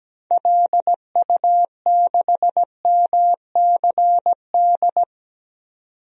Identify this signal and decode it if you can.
Morse code, heard as a keyed tone, about 17 wpm, 700 Hz. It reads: LU6MCD
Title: DXCC-100 Call Signs - 6 Characters - 17wpm